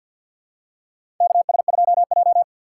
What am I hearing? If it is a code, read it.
Morse code, heard as a keyed tone, about 51 wpm, 700 Hz. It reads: KS2J
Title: USA Call Signs - 4 Characters - 3x Character Spacing - 50wpm